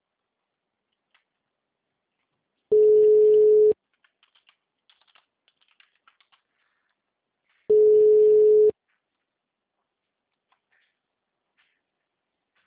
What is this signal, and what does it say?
Ringback tone (Europe, 425 Hz).
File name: out- -2025- - - .1075